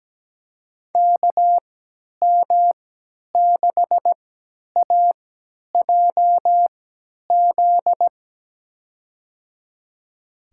Morse code, heard as a keyed tone, about 17 wpm, spaced as for 10 wpm, 700 Hz. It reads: KM6AJZ
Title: USA Call Signs - 3x Character Spacing - 17wpm